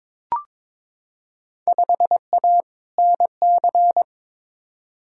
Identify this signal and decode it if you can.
Morse code, heard as a keyed tone, about 22 wpm, 700 Hz. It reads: 5A NC
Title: Field Day Exchanges - 22wpm